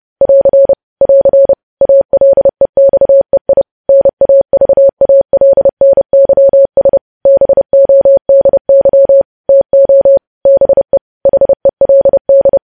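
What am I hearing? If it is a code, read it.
Morse code, heard as a keyed tone, about 30 wpm, 550 Hz. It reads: <AR> <AR> ALEXEI NAVALNYS BODY TO BE HELD
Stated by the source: Morse code news headlines practice, preceded by vvv.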